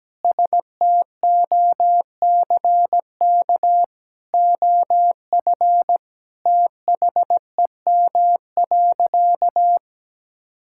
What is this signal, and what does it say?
Morse code, heard as a keyed tone, about 17 wpm, 700 Hz. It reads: STOCK OF THEM.
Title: Sentences from Top 1000 Words - Review Part 5 - 17wpm